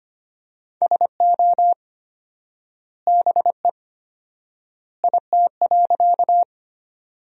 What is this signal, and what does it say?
Morse code, heard as a keyed tone, about 25 wpm, 700 Hz. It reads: SO BE IT.
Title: 4x Word-Spacing - Sentences from Top 100 Words - 25wpm